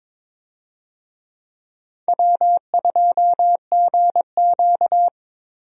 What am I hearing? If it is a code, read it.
Morse code, heard as a keyed tone, about 21 wpm, 700 Hz. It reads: W2GQ
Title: USA Call Signs - 4 Characters - Repeated 3x - 22wpm